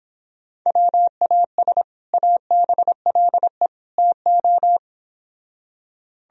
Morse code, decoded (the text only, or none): WAS ABLE TO